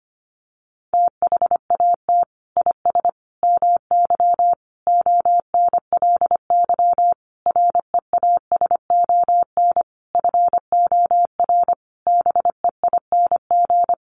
Morse code, heard as a keyed tone, about 25 wpm, 700 Hz. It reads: THAT IS MY ONLY REASON FOR BEING
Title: Sentences from Top 700 Words 25wpm